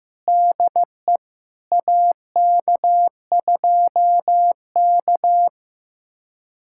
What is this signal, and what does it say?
Morse code, heard as a keyed tone, about 15 wpm, 700 Hz. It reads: DE AK2K
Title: USA Callsign DE Callsign - 4 Characters - 15wpm